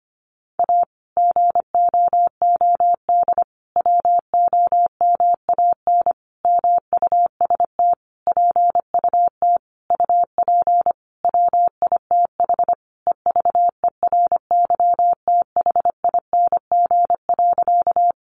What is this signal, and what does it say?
Morse code, heard as a keyed tone, about 25 wpm, 700 Hz. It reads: A GOOD WOMAN MUST PUT UP WITH EVERYTHING.
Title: Sentences from Top 1100 Words 25wpm